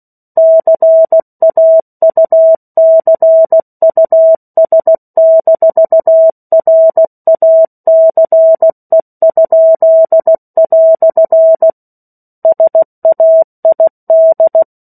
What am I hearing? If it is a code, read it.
Morse code, heard as a keyed tone, about 16 wpm, 650 Hz. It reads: CAUCUS-RACE?" SAID
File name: 03-Chapter-03 16 WPM